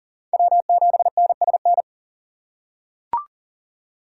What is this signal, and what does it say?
Morse code, heard as a keyed tone, about 40 wpm, 700 Hz. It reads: W7NSN
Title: USA Call Signs - 4x Character Spacing - 40wpm